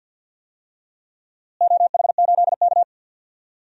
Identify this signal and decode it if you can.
Morse code, heard as a keyed tone, about 47 wpm, 700 Hz. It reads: OH8K
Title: DXCC-100 Call Signs - 4 Characters - Repeated 2x - 50wpm